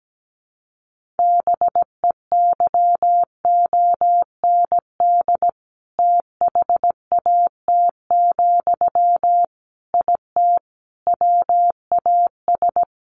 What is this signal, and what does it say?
Morse code, heard as a keyed tone, about 17 wpm, 700 Hz. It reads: BEYOND THAT, IT WAS
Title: Sentences from Top 1500 Words 17wpm